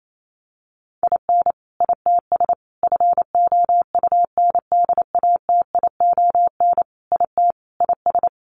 Morse code, heard as a keyed tone, about 28 wpm, 700 Hz. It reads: IN ITS FOUNDATION IT IS
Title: Sentences from Top 1200 Words 28wpm